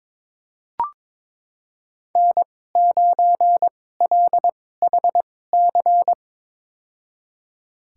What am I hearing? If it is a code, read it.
Morse code, heard as a keyed tone, about 22 wpm, spaced as for 16 wpm, 700 Hz. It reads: N9LHC
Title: USA Call Signs - 5 Characters - 2x Character Spacing - 22wpm